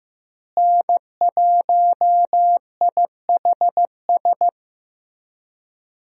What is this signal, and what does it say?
Morse code, heard as a keyed tone, about 15 wpm, 700 Hz. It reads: N1IHS
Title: USA Call Signs - 5 Characters - Repeated 3x - 15wpm